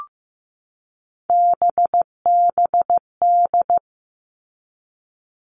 Morse code, heard as a keyed tone, about 15 wpm, 700 Hz. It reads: BBD
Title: All Cut Numbers - 3-Digit 15wpm